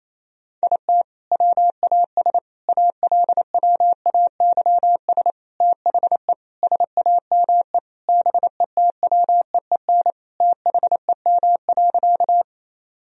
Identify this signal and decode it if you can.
Morse code, heard as a keyed tone, about 28 wpm, 700 Hz. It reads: IT WAS ALWAYS THE SAME BETWEEN THEM.
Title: Sentences from Top 1000 Words - Review Part 1 - 28wpm